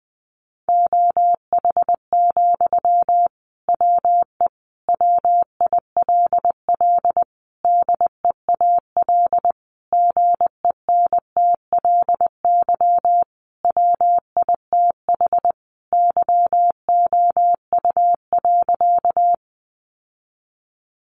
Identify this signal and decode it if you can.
Morse code, heard as a keyed tone, about 20 wpm, 700 Hz. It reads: OH, WE WILL DEAL GENTLY WITH YOU.
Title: Sentences from Top 1500 Words 20wpm